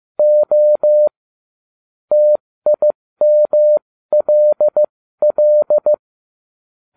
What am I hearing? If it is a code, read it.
Morse code, heard as a keyed tone, about 15 wpm, spaced as for 13 wpm, 600 Hz. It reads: O TIMLL